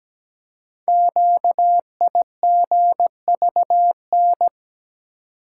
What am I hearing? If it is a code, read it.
Morse code, heard as a keyed tone, about 17 wpm, 700 Hz. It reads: QIGVN
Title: Code Groups - 5 Letters - 17wpm